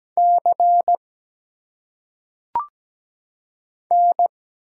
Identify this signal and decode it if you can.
Morse code, heard as a keyed tone, about 17 wpm, 700 Hz. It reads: C N
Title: Single Letters - Phonetic Pronunciation 17wpm